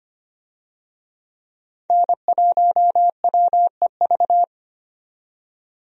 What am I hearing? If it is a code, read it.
Morse code, heard as a keyed tone, about 25 wpm, 700 Hz. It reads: N1WEV
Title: USA Call Signs - 5 Characters - 25wpm